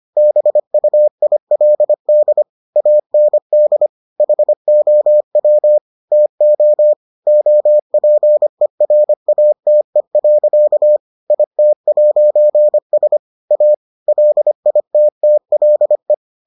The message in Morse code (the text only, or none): BUILD AND HOW TO OPERATE. IT'S A LITTLE